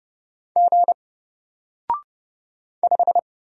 Morse code, heard as a keyed tone, about 30 wpm, 700 Hz. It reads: G 5
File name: Single-Letter-Number-30wpm